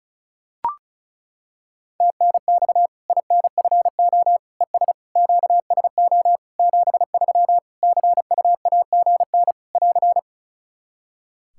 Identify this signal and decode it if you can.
Morse code, heard as a keyed tone, about 36 wpm, 700 Hz. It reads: TNX INFO ES QSO 73 CUAGN <AR>
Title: Practice QSO - Exchange 4 - Ending - 35wpm